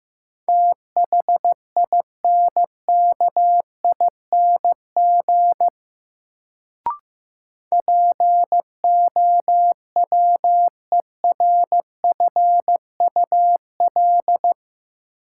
Morse code, heard as a keyed tone, about 15 wpm, 700 Hz. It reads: THINKING POWERFUL
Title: Sentences from Top 1000 Words - Review Part 6 - New Word Drill - 15wpm